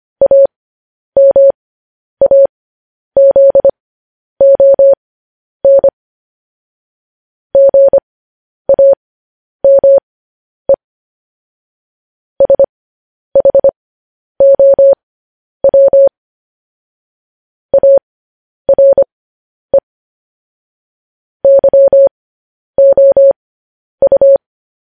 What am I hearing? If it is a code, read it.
Morse code, heard as a keyed tone, about 25 wpm, spaced as for 10 wpm, 550 Hz. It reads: AMAZON GAME SHOW ARE YOU
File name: podcast / News headlines-1x-25x10